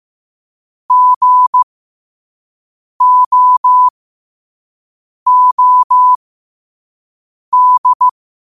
Morse code, heard as a keyed tone, about 15 wpm, spaced as for 5 wpm, 1000 Hz. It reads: GOOD
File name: qotdfiles / 5wpm